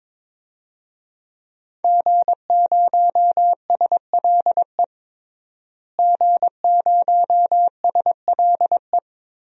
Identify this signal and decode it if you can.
Morse code, heard as a keyed tone, about 21 wpm, 700 Hz. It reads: G0SLE G0SLE
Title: DXCC-100 Call Signs - 5 Characters - Repeated 2x - 22wpm